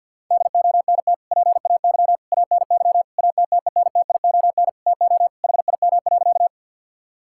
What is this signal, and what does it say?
Morse code, heard as a keyed tone, about 52 wpm, 700 Hz. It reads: DONT PAY ANY ATTENTION TO HIM.
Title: Sentences from Top 1000 Words - Review Part 6 - 50wpm